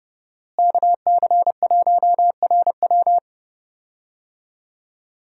Morse code, heard as a keyed tone, about 30 wpm, 700 Hz. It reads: KC1RW
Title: USA Call Signs - 5 Characters - Repeated 2x - 30wpm